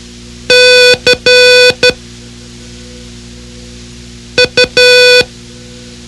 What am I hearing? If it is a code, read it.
Morse code, heard as a keyed tone, about 12 wpm, spaced as for 3 wpm, 500 Hz. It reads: CU
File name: ham radio / CodeQuick / Code Quick 04